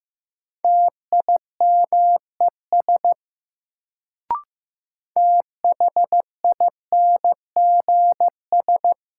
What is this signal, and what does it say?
Morse code, heard as a keyed tone, about 15 wpm, 700 Hz. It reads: TIMES THINGS
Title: Sets of 3 Words - New Word Drill - 15wpm